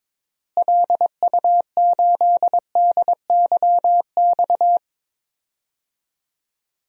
Morse code, heard as a keyed tone, about 22 wpm, 700 Hz. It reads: LU8DYX
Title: DXCC-100 Call Signs - 6 Characters - Repeated 3x - 22wpm